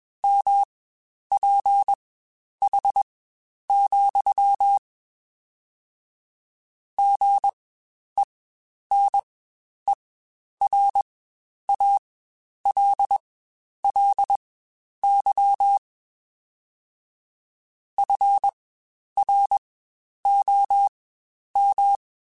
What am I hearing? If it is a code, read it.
Morse code, heard as a keyed tone, about 21 wpm, spaced as for 10 wpm, 800 Hz. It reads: MPH, GENERALLY FROM
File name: Spaced Characters / Spaced Characters-mp3s / Weather 4 21 sp @20